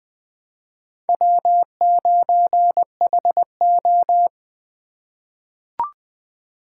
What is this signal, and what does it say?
Morse code, heard as a keyed tone, about 20 wpm, 700 Hz. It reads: W9HO
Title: USA Call Signs - 4 Characters - 3x Character Spacing - 20wpm